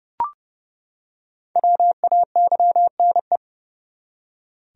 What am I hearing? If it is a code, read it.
Morse code, heard as a keyed tone, about 31 wpm, 700 Hz. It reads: WAYNE
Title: USA Common Names - 30wpm